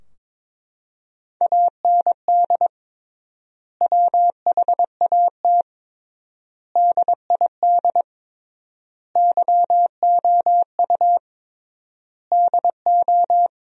Morse code, heard as a keyed tone, about 23 wpm, 700 Hz. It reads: AND WHAT DID YOU DO
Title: 3x Word-Spacing - Sentences from Top 100 Words - 22wpm